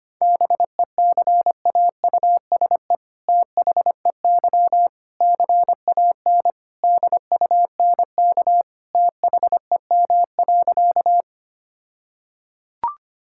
Morse code, heard as a keyed tone, about 25 wpm, 700 Hz. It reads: BECAUSE THEY CAN DUNK THEM.
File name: practice / 25wpm / All-Age Appropriate Jokes - 25wpm